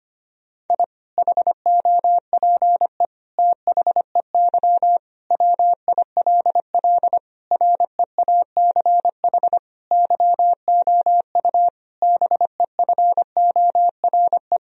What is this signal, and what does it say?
Morse code, heard as a keyed tone, about 25 wpm, 700 Hz. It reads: I HOPE THEY WILL REACH YOU BEFORE
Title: Sentences from Top 1000 Words - Review Part 2 - 25wpm